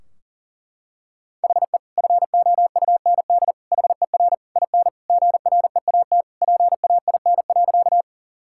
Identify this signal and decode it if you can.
Morse code, heard as a keyed tone, about 38 wpm, 700 Hz. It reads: HE FOUND HER IN GREAT PAIN.
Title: Sentences from Top 1600 Words 40wpm